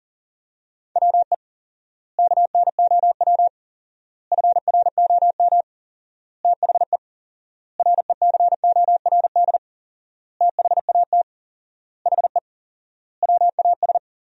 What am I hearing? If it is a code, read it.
Morse code, heard as a keyed tone, about 40 wpm, 700 Hz. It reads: WE KNOW FROM THE RECORD THAT HE WAS
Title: 4x Word-Spacing - Sentences from Top 500 Words - 40wpm